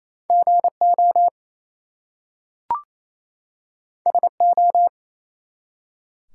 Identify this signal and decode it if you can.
Morse code, heard as a keyed tone, about 27 wpm, 700 Hz. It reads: GO SO
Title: Two Letter Words - 28wpm